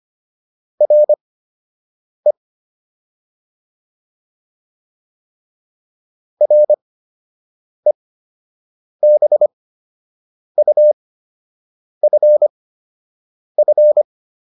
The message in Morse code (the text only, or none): RE REBUFF